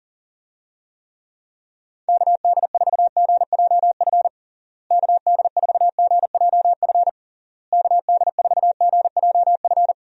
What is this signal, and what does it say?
Morse code, heard as a keyed tone, about 40 wpm, 700 Hz. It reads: KD4GJF KD4GJF KD4GJF
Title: USA Call Signs - 6 Characters - Repeated 3x - 40wpm